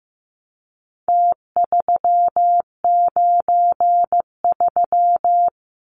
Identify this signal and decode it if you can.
Morse code, heard as a keyed tone, about 15 wpm, 700 Hz. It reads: T393
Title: Numbers - Contest Style - 4-Digit 15wpm